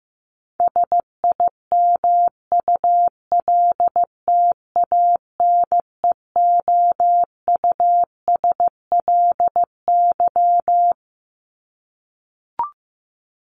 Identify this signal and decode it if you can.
Morse code, heard as a keyed tone, about 15 wpm, 700 Hz. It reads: SIMULTANEOUSLY
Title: LY Suffix - Top 100 Words with Base Words 15wpm